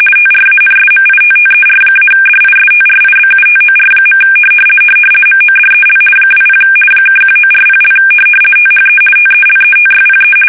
RTTY, about 50 baud. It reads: TRANSMITTER-DISTRIBUTOR. MODEL 20 (1940) THE TELETYPE
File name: fsk850-50B / teletype-corp